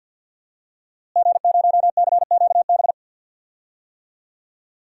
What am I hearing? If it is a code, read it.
Morse code, heard as a keyed tone, about 53 wpm, 700 Hz. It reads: G0CQB
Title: DXCC-100 Call Signs - 5 Characters - Repeated 2x - 50wpm